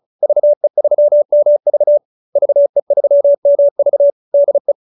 Morse code, heard as a keyed tone, about 34 wpm, 570 Hz. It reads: VE3MV VE3MV DE